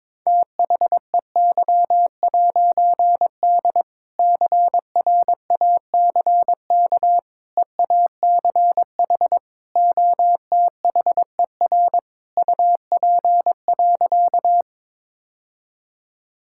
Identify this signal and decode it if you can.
Morse code, heard as a keyed tone, about 22 wpm, 700 Hz. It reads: THEY'D CRACK EACH OTHER UP.
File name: All-Age Appropriate Jokes - 22wpm